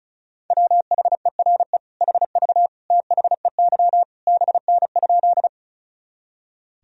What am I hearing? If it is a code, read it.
Morse code, heard as a keyed tone, about 36 wpm, 700 Hz. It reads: WHERE HV THEY BN?
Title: Sentences from Top 200 Words and Abbreviations 35wpm